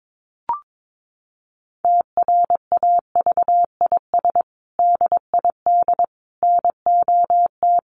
Morse code, heard as a keyed tone, about 22 wpm, 700 Hz. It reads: TRAVIS DID NOT
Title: A Sound Of Thunder 22wpm